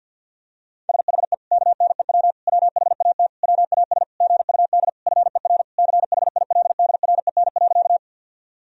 Morse code, heard as a keyed tone, about 46 wpm, 700 Hz. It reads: SHE KNEW WHAT WAS GUD FER CHILDREN.
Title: Sentences from Top 400 Words and Abbreviations 50wpm